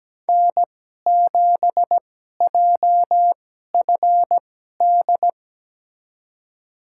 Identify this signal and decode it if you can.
Morse code, heard as a keyed tone, about 17 wpm, spaced as for 12 wpm, 700 Hz. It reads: N7JFD
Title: USA Call Signs - 5 Characters - 2x Character Spacing - 17wpm